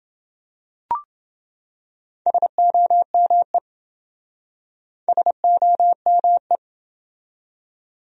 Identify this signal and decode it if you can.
Morse code, heard as a keyed tone, about 27 wpm, 700 Hz. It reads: SOME SOME